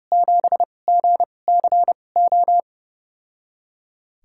Morse code, heard as a keyed tone, about 30 wpm, spaced as for 22 wpm, 700 Hz. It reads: 7GCO
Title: USA Call Signs - 6 Characters - 2x Character Spacing - 30wpm